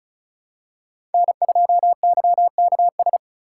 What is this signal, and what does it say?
Morse code, heard as a keyed tone, about 34 wpm, 700 Hz. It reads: N2YKS
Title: USA Call Signs - 5 Characters - 4x Character Spacing - 35wpm